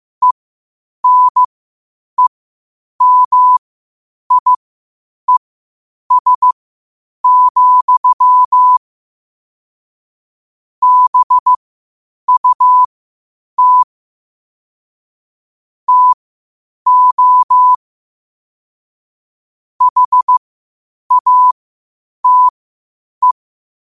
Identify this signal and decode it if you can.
Morse code, heard as a keyed tone, about 15 wpm, spaced as for 8 wpm, 1000 Hz. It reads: ENEMIES, BUT TO HATE
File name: qotdfiles / 8wpm